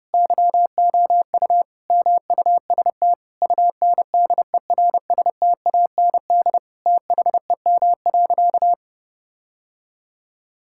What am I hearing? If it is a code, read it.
Morse code, heard as a keyed tone, about 30 wpm, 700 Hz. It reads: YOU MUST UNDERSTAND THEM.
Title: Sentences from Top 1000 Words - Review Part 5 - 30wpm